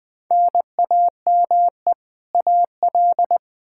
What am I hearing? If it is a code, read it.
Morse code, heard as a keyed tone, about 20 wpm, 700 Hz. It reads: NAME AL
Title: Practice QSO - Exchange 1 - Beginning - 20wpm